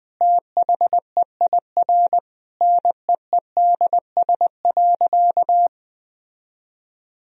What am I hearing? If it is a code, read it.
Morse code, heard as a keyed tone, about 20 wpm, 700 Hz. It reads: THEIR NEEDS.
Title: Sentences from Top 1000 Words - Review Part 3 - 20wpm